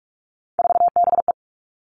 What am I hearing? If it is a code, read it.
Morse code, heard as a keyed tone, about 44 wpm, 700 Hz. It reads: 4BE
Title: All Cut Numbers - 3-Digit 45wpm